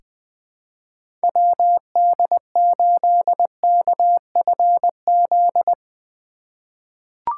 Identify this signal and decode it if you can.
Morse code, heard as a keyed tone, about 21 wpm, 700 Hz. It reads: WD8KFZ
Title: USA Call Signs - 6 Characters - 3x Character Spacing - 20wpm